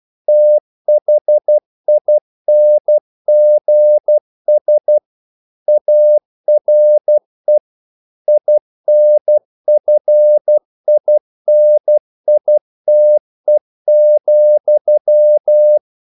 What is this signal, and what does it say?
Morse code, heard as a keyed tone, about 12 wpm, 600 Hz. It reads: THINGS ARE INFINITE,
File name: CW / Practice / EnglishQuotations0112